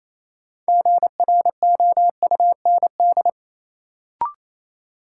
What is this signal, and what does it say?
Morse code, heard as a keyed tone, about 28 wpm, 700 Hz. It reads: GROUND
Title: Sets of 3 Words - New Word Drill - 28wpm